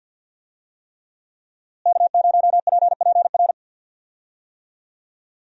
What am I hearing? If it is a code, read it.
Morse code, heard as a keyed tone, about 46 wpm, 700 Hz. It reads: K0PPR
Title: USA Call Signs - 5 Characters - Repeated 3x - 50wpm